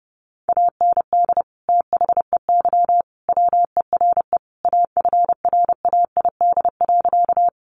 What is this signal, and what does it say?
Morse code, heard as a keyed tone, about 30 wpm, 700 Hz. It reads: AND THEY WERE AFRAID.
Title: Sentences from Top 1500 Words 30wpm